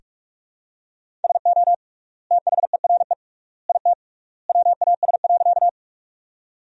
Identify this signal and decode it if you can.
Morse code, heard as a keyed tone, about 46 wpm, 700 Hz. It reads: SO THERE IT WAS.
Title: 3x Word-Spacing - Sentences from Top 100 Words - 45wpm